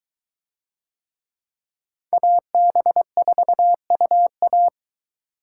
Morse code, heard as a keyed tone, about 22 wpm, 700 Hz. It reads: AB4UA